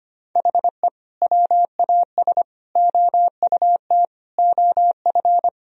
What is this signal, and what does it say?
Morse code, heard as a keyed tone, about 25 wpm, 700 Hz. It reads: HE WAS OUT OF
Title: Sentences from Top 1000 Words - Review Part 4 - 25wpm